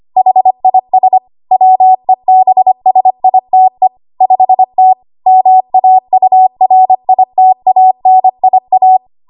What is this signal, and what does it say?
Morse code, heard as a keyed tone, about 25 wpm, 750 Hz. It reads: HIS WEBSITE 5T MAURITANIA
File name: Archive / 25 WPM / 25WPM